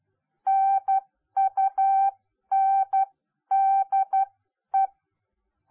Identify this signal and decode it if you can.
Morse code, heard as a keyed tone, about 12 wpm, spaced as for 10 wpm, 780 Hz. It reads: NUNDE